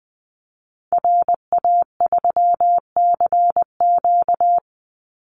Morse code, heard as a keyed tone, about 19 wpm, 700 Hz. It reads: RA3CQ
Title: Call Signs Repeated 3x - 20wpm